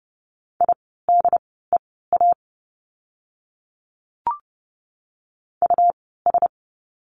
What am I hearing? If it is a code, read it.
Morse code, heard as a keyed tone, about 30 wpm, spaced as for 17 wpm, 700 Hz. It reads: IDEA US
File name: course / 30:10wpm / 108 Lesson 18f - Letter m - Words 30wpm